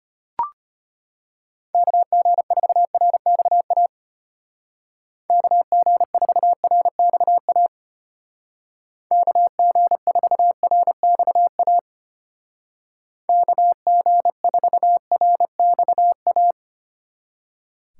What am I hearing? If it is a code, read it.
Morse code, heard as a keyed tone, about 31 wpm, 700 Hz. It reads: KG4RXA KG4RXA KG4RXA KG4RXA